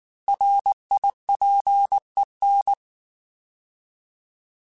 Morse code, heard as a keyed tone, about 19 wpm, 800 Hz. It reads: RIPEN
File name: Alphabetical / Alphabetical mp3s / Rs @18